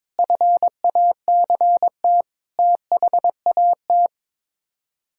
Morse code, heard as a keyed tone, about 22 wpm, 700 Hz. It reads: FACT THAT
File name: practice / 22wpm / Sets of 7 Words - 22wpm